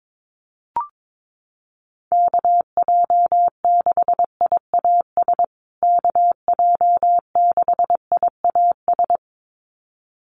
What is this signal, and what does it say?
Morse code, heard as a keyed tone, about 22 wpm, 700 Hz. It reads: KJ6IAS KJ6IAS
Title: USA Call Signs Repeated 2x 22wpm